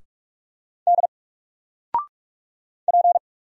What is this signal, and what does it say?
Morse code, heard as a keyed tone, about 46 wpm, 700 Hz. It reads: D P
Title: Single Letter-Number 45wpm